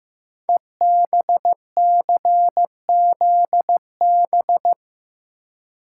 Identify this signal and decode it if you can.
Morse code, heard as a keyed tone, about 15 wpm, 700 Hz. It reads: EBCZB
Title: Code Groups - 5 Letters - 15wpm